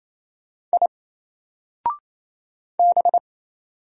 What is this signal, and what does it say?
Morse code, heard as a keyed tone, about 27 wpm, 700 Hz. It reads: I B
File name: Single-Letter-Number-28wpm